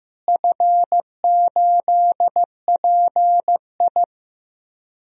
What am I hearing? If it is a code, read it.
Morse code, heard as a keyed tone, about 15 wpm, 700 Hz. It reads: F8PI
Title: DXCC-100 Call Signs - 4 Characters - Repeated 2x - 15wpm